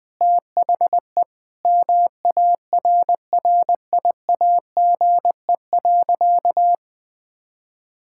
Morse code, heard as a keyed tone, about 20 wpm, 700 Hz. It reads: THE MARRIAGE.
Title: Sentences from Top 1000 Words - Review Part 5 - 20wpm